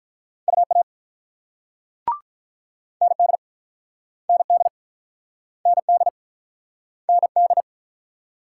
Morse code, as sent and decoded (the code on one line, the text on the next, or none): ..- .- / -. -.. / -. -.. / -. -.. / -. -..
UA ND ND ND ND